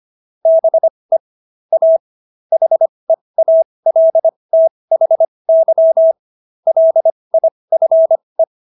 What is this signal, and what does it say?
Morse code, heard as a keyed tone, about 25 wpm, spaced as for 20 wpm, 650 Hz. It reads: BE A HEALTHY LIFE